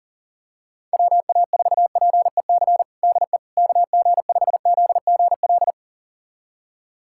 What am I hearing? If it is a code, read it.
Morse code, heard as a keyed tone, about 40 wpm, 700 Hz. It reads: WA4PEC DE KG5ZGL
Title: USA Callsign DE Callsign - 6 Characters - 40wpm